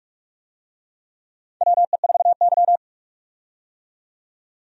Morse code, heard as a keyed tone, about 44 wpm, 700 Hz. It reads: WE4Y
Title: USA Call Signs - 4 Characters - 45wpm